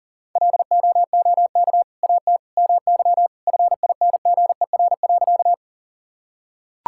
Morse code, heard as a keyed tone, about 37 wpm, 700 Hz. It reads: LOOK AT MY FINGER.
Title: Sentences from Top 1600 Words 40wpm